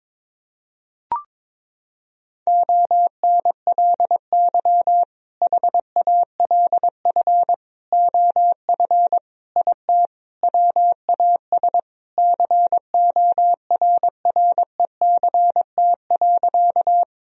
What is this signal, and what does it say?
Morse code, heard as a keyed tone, about 22 wpm, 700 Hz. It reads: ONLY HALF OF IT WAS CORRECT.
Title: Sentences from Top 1000 Words - Review Part 2 - 22wpm